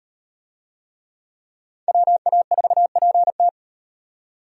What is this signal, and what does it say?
Morse code, heard as a keyed tone, about 37 wpm, 700 Hz. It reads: WA4PT